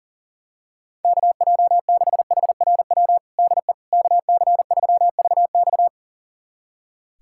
Morse code, heard as a keyed tone, about 37 wpm, 700 Hz. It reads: KJ6HRW DE KC3VX
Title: USA Callsign DE Callsign 40wpm